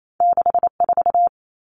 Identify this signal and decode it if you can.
Morse code, heard as a keyed tone, about 28 wpm, 700 Hz. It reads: 6 4
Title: Common Cut Numbers - 2-Digit 28wpm